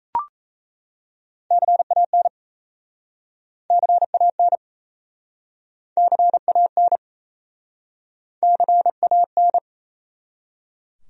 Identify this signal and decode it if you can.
Morse code, heard as a keyed tone, about 37 wpm, 700 Hz. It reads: CAN CAN CAN CAN